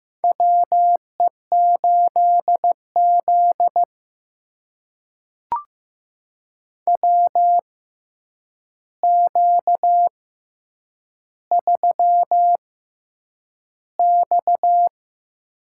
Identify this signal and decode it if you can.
Morse code, heard as a keyed tone, about 15 wpm, spaced as for 8 wpm, 700 Hz. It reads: WE8Z WQ3X
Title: USA Call Signs - 4 Characters - 6x Character Spacing - 15wpm